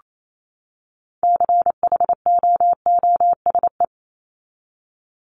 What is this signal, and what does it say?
Morse code, heard as a keyed tone, about 28 wpm, 700 Hz. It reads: CHOOSE
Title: Top 501-750 Words 28wpm